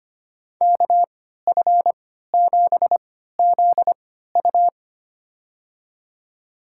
Morse code, heard as a keyed tone, about 25 wpm, spaced as for 14 wpm, 700 Hz. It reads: KF7ZU
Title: USA Call Signs - 5 Characters - 3x Character Spacing - 25wpm